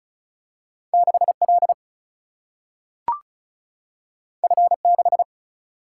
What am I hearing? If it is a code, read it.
Morse code, heard as a keyed tone, about 36 wpm, 700 Hz. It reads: 6L F6
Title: Single-Letter Single-Number 35wpm